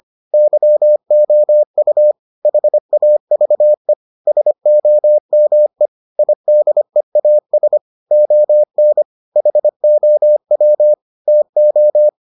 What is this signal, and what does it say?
Morse code, heard as a keyed tone, about 25 wpm, 600 Hz. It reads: YOU HAVE SOME IDEAS ON HOW TO